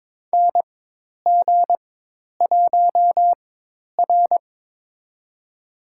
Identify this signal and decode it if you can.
Morse code, heard as a keyed tone, about 22 wpm, spaced as for 10 wpm, 700 Hz. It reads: NG1R
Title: USA Call Signs - 4 Characters - 4x Character Spacing - 22wpm